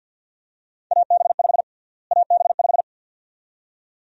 Morse code, heard as a keyed tone, about 50 wpm, 700 Hz. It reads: AB5 AB5
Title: USA Call Sign Prefixes - Repeated 2x - 50wpm